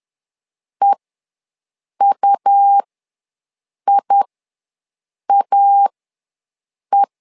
Morse code, heard as a keyed tone, about 11 wpm, spaced as for 6 wpm, 800 Hz. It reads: EUIAE
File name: LEZIONI / 2 UHDN / 2 U ws 10wpm 2017